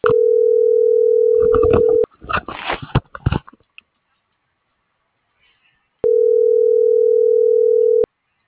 Ringback tone (North America), 440 + 480 Hz.